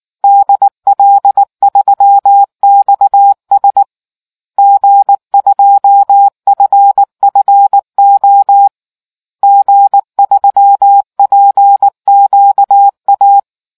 Morse code, heard as a keyed tone, about 19 wpm, 800 Hz. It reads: DL3XS G2FFO G3PQA